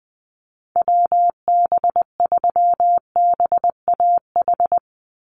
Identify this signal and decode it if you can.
Morse code, heard as a keyed tone, about 20 wpm, 700 Hz. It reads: WB3BAH
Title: USA Call Signs 20wpm